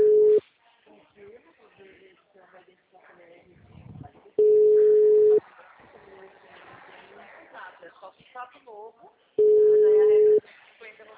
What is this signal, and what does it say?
Ringback tone (Europe, 425 Hz), with speech.
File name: out- -2011- - - .13